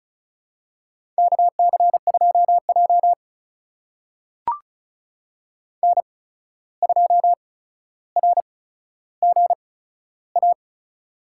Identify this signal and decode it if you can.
Morse code, heard as a keyed tone, about 36 wpm, spaced as for 14 wpm, 700 Hz. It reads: KC2J N2RGA
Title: USA Call Signs - 8x Character Spacing - 35wpm